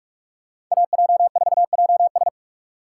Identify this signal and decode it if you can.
Morse code, heard as a keyed tone, about 46 wpm, 700 Hz. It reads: AJ4JS
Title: USA Call Signs - 5 Characters - 4x Character Spacing - 45wpm